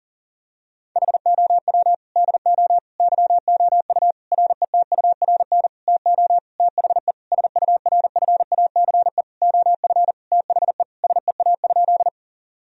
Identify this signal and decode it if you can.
Morse code, heard as a keyed tone, about 40 wpm, 700 Hz. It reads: HOW DO YOU RETURN TO THE SURFACE OF THE SEA?
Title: Sentences from Top 1000 Words - Review Part 3 - 40wpm